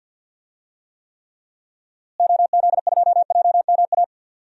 Morse code, heard as a keyed tone, about 49 wpm, 700 Hz. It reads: OZ2JMA
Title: DXCC-100 Call Signs - 6 Characters - Repeated 3x - 50wpm